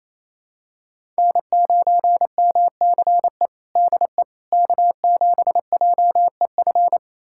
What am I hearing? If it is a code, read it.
Morse code, heard as a keyed tone, about 27 wpm, 700 Hz. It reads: N9MCE DE K7JEF
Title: USA Callsign DE Callsign - 5 Characters - 28wpm